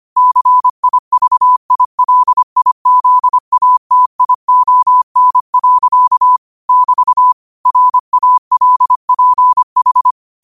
Morse code, heard as a keyed tone, about 25 wpm, 1000 Hz. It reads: CIVILIZATION. <BT> RALPH